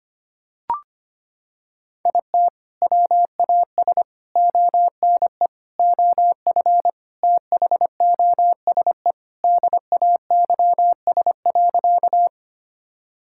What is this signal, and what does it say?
Morse code, heard as a keyed tone, about 25 wpm, 700 Hz. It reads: IT WAS ONE OF THOSE DAYS.
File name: practice / 25wpm / Sentences with 4-Word Phrases - 25wpm